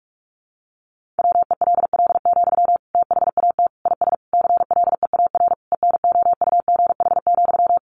Morse code, heard as a keyed tone, about 44 wpm, 700 Hz. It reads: WELL, THAT IS CLEAR ENOUGH,
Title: Sentences from Top 900 Words 45wpm